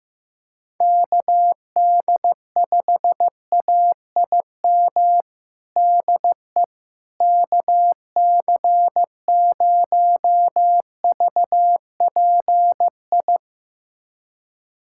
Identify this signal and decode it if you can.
Morse code, heard as a keyed tone, about 15 wpm, 700 Hz. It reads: KD5AIM DE KC0VPI
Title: USA Callsign DE Callsign 15wpm